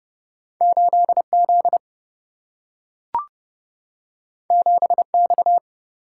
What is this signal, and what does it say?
Morse code, heard as a keyed tone, about 30 wpm, 700 Hz. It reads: 8 Z 7 X
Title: Single-Letter Single-Number 30wpm